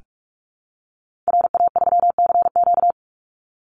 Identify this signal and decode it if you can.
Morse code, heard as a keyed tone, about 44 wpm, 700 Hz. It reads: RA3CQ
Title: Call Signs Repeated 3x - 45wpm